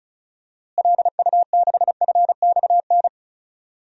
Morse code, heard as a keyed tone, about 36 wpm, 700 Hz. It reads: LU6FXN
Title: DXCC-100 Call Signs - 6 Characters - Repeated 2x - 35wpm